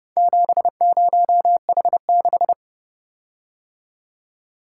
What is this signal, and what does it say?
Morse code, heard as a keyed tone, about 30 wpm, 700 Hz. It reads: 70H6